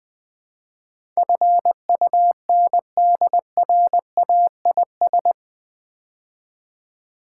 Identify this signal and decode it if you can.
Morse code, heard as a keyed tone, about 20 wpm, 700 Hz. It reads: FUNDRAIS
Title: Build A Word - 4 Letters At A Time - 11-Letter Words - 20wpm